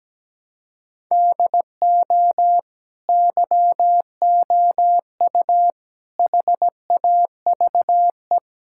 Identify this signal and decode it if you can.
Morse code, heard as a keyed tone, about 17 wpm, 700 Hz. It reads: DO YOU HAVE
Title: Sentences from Top 1000 Words - Review Part 5 - 17wpm